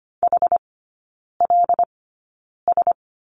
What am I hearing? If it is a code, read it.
Morse code, heard as a keyed tone, about 25 wpm, spaced as for 9 wpm, 700 Hz. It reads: HLS
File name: course / 25:10wpm / 093 Lesson 16e - Letter c - Three Characters 25wpm